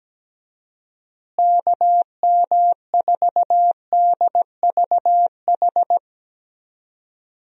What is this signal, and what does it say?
Morse code, heard as a keyed tone, about 17 wpm, 700 Hz. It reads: KM4DVH
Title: USA Call Signs - 6 Characters - Repeated 3x - 17wpm